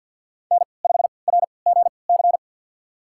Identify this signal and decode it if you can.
Morse code, heard as a keyed tone, about 53 wpm, spaced as for 29 wpm, 700 Hz. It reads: N5RGC
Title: USA Call Signs - 5 Characters - 3x Character Spacing - 50wpm